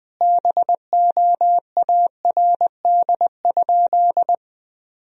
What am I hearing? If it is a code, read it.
Morse code, heard as a keyed tone, about 20 wpm, 700 Hz. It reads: BOARD?
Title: Sentences from Top 1000 Words - Review Part 4 - 20wpm